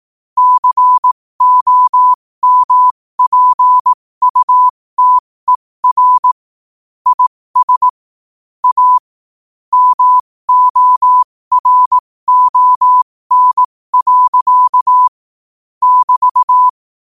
Morse code, heard as a keyed tone, about 18 wpm, spaced as for 15 wpm, 1000 Hz. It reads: COMPUTER IS A MORON. <BT>
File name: qotdfiles / 15wpm